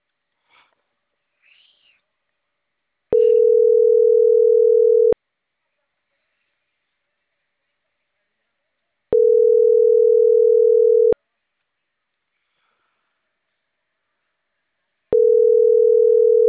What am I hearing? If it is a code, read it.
Ringback tone (North America), 440 + 480 Hz.